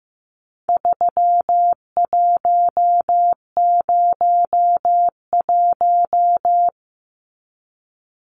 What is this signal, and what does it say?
Morse code, heard as a keyed tone, about 15 wpm, 700 Hz. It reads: 3101
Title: Numbers - 4-Digit 15wpm